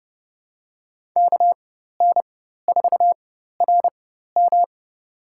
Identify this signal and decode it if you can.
Morse code, heard as a keyed tone, about 32 wpm, spaced as for 14 wpm, 700 Hz. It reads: KN4RM
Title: USA Call Signs - 4x Character Spacing - 30wpm